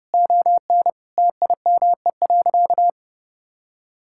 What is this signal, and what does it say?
Morse code, heard as a keyed tone, about 30 wpm, 700 Hz. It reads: ON TIME.
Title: Sentences from Top 1000 Words - Review Part 6 - 30wpm